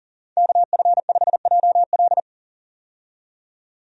Morse code, heard as a keyed tone, about 40 wpm, 700 Hz. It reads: KF5JL
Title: USA Call Signs - 5 Characters - 40wpm